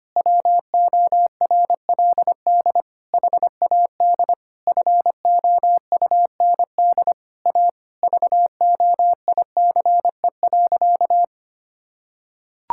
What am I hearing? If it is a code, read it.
Morse code, heard as a keyed tone, about 25 wpm, 700 Hz. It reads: WORLD HAD FOUND A VOICE.
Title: Sentences from Top 1600 Words 25wpm